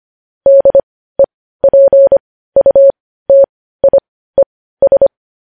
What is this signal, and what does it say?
Morse code, heard as a keyed tone, about 25 wpm, spaced as for 15 wpm, 550 Hz. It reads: DEPUTIES
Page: news-headlines-in-morse-code-at-15-wpm